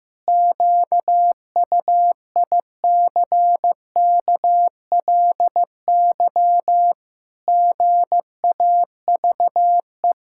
Morse code, heard as a keyed tone, about 16 wpm, 700 Hz. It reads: QUICKLY GAVE
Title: Pangram Challenge v2 - 15wpm